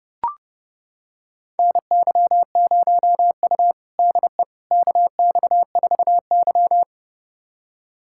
Morse code, heard as a keyed tone, about 32 wpm, 700 Hz. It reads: NY0U DE KX4Y
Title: USA Callsign DE Callsign - 4 Characters - 30wpm